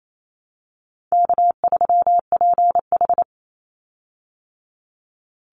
Morse code, heard as a keyed tone, about 28 wpm, 700 Hz. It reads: K3PH
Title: Easy Call Signs Repeated 2x - 28wpm